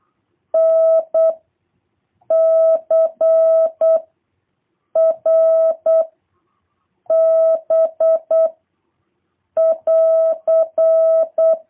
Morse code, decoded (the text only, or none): N C R B <AR>